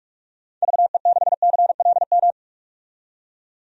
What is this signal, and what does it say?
Morse code, heard as a keyed tone, about 46 wpm, 700 Hz. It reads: VE6CLM
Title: DXCC-100 Call Signs - 6 Characters - 45wpm